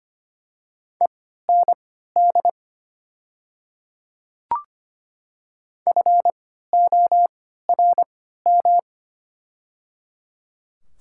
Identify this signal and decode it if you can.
Morse code, heard as a keyed tone, about 25 wpm, spaced as for 14 wpm, 700 Hz. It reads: END FORM
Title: 3x Character Spacing - Top 500 Words - 25wpm